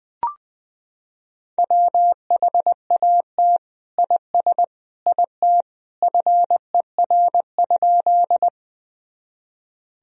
Morse code, heard as a keyed tone, about 20 wpm, 700 Hz. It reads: WHAT IS IT FER?
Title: Sentences from Top 100 Words and Abbreviations 20wpm